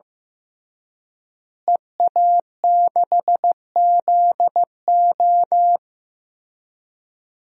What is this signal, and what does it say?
Morse code, heard as a keyed tone, about 15 wpm, 700 Hz. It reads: EA6ZO
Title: DXCC-100 Call Signs - 5 Characters - Repeated 3x - 15wpm